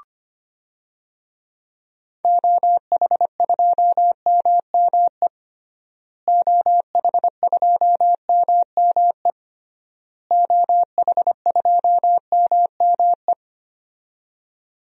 Morse code, heard as a keyed tone, about 25 wpm, 700 Hz. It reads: OH2MME OH2MME OH2MME
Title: DXCC-100 Call Signs - 6 Characters - Repeated 3x - 25wpm